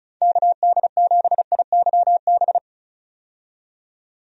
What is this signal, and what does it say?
Morse code, heard as a keyed tone, about 34 wpm, 700 Hz. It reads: KD7IYB
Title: USA Call Signs - 6 Characters - 35wpm